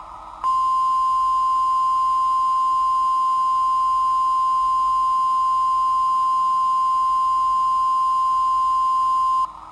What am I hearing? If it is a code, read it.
Emergency Alert System (SAME) tones (the 1050 Hz attention tone). It is heard from 0.4 to 9.4 s.
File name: sitoariold / beacon hk